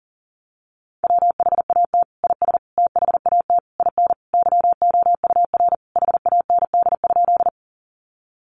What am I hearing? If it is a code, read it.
Morse code, heard as a keyed tone, about 40 wpm, 700 Hz. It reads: WHAT IS THAT IN YOUR HAND?
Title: Sentences from Top 200 Words 40wpm